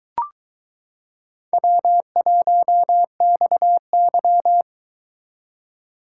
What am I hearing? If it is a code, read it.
Morse code, heard as a keyed tone, about 23 wpm, 700 Hz. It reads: W1XY